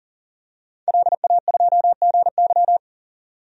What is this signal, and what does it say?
Morse code, heard as a keyed tone, about 40 wpm, 700 Hz. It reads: LA2GY
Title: DXCC-100 Call Signs - 5 Characters - 40wpm